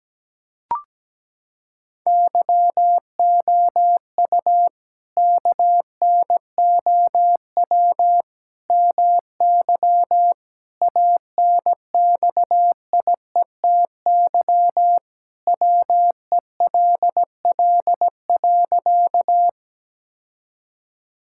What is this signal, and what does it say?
Morse code, heard as a keyed tone, about 17 wpm, 700 Hz. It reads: YOU KNOW MY ANXIETY WELL.
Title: Sentences from Top 1600 Words 17wpm